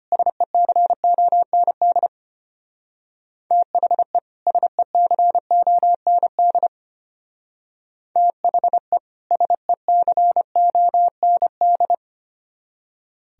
Morse code, heard as a keyed tone, about 31 wpm, 700 Hz. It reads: SECOND THE SECOND THE SECOND
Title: Speed-Racing - Sets of 2 Words 25wpm